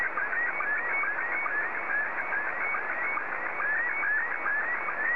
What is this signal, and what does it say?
SSTV (slow-scan television), Scottie 1.